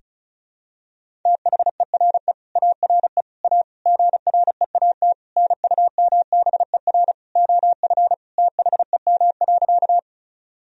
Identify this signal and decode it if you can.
Morse code, heard as a keyed tone, about 36 wpm, 700 Hz. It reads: THERE ARE A GREAT NUMBER OF THEM.
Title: Sentences from Top 1000 Words - Review Part 1 - 35wpm